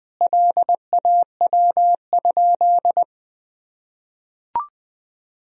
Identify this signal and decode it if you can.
Morse code, heard as a keyed tone, about 21 wpm, 700 Hz. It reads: LAW?
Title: Sentences from Top 1000 Words - Review Part 4 - 20wpm